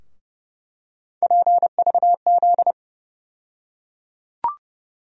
Morse code, heard as a keyed tone, about 32 wpm, 700 Hz. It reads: PVZ
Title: USA Call Sign Postfixes - 3 Letters 30wpm